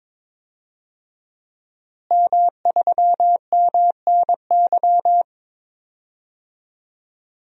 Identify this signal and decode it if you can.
Morse code, heard as a keyed tone, about 22 wpm, 700 Hz. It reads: M3MNY
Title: DXCC-100 Call Signs - 5 Characters - Repeated 3x - 22wpm